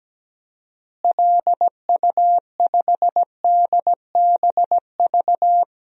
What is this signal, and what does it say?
Morse code, heard as a keyed tone, about 17 wpm, 700 Hz. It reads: LU5DBV
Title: DXCC-100 Call Signs - 6 Characters - Repeated 3x - 17wpm